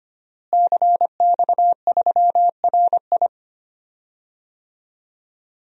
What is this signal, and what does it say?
Morse code, heard as a keyed tone, about 25 wpm, 700 Hz. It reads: CX3RI
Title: DXCC-100 Call Signs - 5 Characters - Repeated 3x - 25wpm